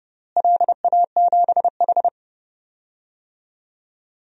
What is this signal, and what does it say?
Morse code, heard as a keyed tone, about 29 wpm, 700 Hz. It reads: LA7H
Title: DXCC-100 Call Signs - 4 Characters - Repeated 3x - 30wpm